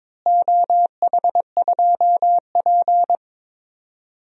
Morse code, heard as a keyed tone, about 22 wpm, 700 Hz. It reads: OH2P
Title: DXCC-100 Call Signs - 4 Characters - Repeated 3x - 22wpm